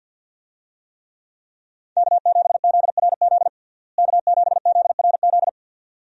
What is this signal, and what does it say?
Morse code, heard as a keyed tone, about 51 wpm, 700 Hz. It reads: K7ZRZ K7ZRZ
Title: USA Call Signs - 5 Characters - Repeated 2x - 50wpm